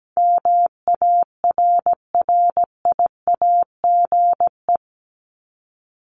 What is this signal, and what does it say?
Morse code, heard as a keyed tone, about 17 wpm, 700 Hz. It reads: MARRIAGE
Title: Top 751-1000 Words 17wpm